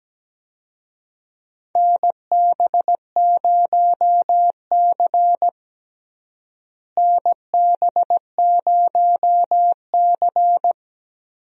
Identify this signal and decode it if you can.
Morse code, heard as a keyed tone, about 17 wpm, 700 Hz. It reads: NB0C NB0C
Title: USA Call Signs - 4 Characters - Repeated 3x - 17wpm